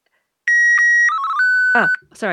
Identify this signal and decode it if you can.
SSTV (slow-scan television), Martin 1, with speech.